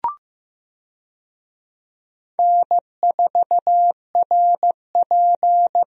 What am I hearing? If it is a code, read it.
Morse code, heard as a keyed tone, about 16 wpm, 700 Hz. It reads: N4RP
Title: USA Call Signs - 4 Characters - Repeated 3x - 15wpm